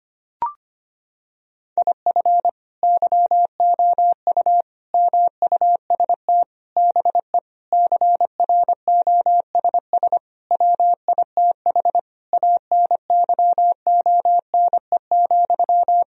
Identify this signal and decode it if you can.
Morse code, heard as a keyed tone, about 25 wpm, 700 Hz. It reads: IF YOU MUST BE CROSS WITH ANYONE,
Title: Sentences from Top 1000 Words - Review Part 1 - 25wpm